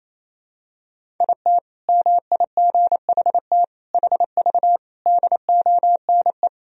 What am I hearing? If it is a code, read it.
Morse code, heard as a keyed tone, about 28 wpm, 700 Hz. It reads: IT MIGHT HV DONE
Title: Sentences from Top 500 Words and Abbreviations 28wpm